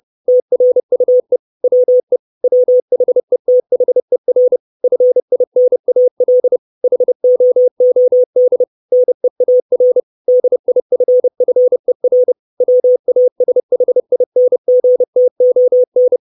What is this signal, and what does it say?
Morse code, heard as a keyed tone, about 30 wpm, 500 Hz. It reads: TRUE WE WHETHER FINAL HOOD NEAR DIFFER WASHINGTON
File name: 632 Words 30WPM